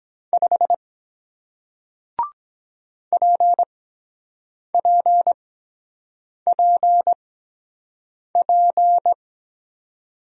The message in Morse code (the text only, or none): 5 PPPP